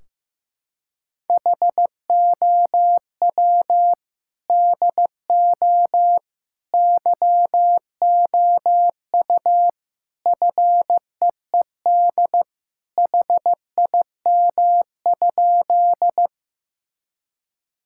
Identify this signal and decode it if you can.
Morse code, heard as a keyed tone, about 15 wpm, 700 Hz. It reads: HOW DO YOU FEED HIM?
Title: Sentences from Top 1600 Words 15wpm